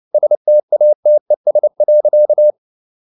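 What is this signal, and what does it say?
Morse code, heard as a keyed tone, about 29 wpm, 600 Hz. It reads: STATES.
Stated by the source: Morse code at 29 wpm, sending Ken Follett's novel Never in English, from CW med Gnister.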